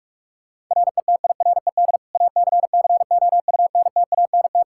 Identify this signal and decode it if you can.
Morse code, heard as a keyed tone, about 45 wpm, 700 Hz. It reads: RETIRED ACCOUNTANT
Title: Practice QSO - Exchange 3 - More Info - 45wpm